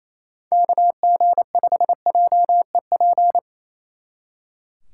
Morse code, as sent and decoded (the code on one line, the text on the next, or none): -.- --. ..... .--- . .--.
KG5JEP